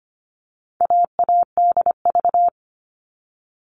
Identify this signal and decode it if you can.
Morse code, heard as a keyed tone, about 25 wpm, 700 Hz. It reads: AADV
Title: All Cut Numbers - 4-Digit 25wpm